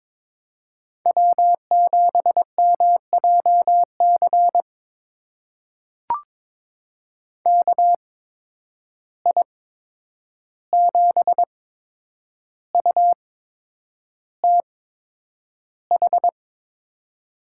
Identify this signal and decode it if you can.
Morse code, heard as a keyed tone, about 22 wpm, spaced as for 9 wpm, 700 Hz. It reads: W7MJC KI7UTH
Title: USA Call Signs - 8x Character Spacing - 22wpm